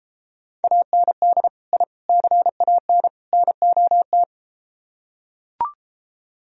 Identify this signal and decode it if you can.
Morse code, heard as a keyed tone, about 32 wpm, 700 Hz. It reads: AND I CAN NOT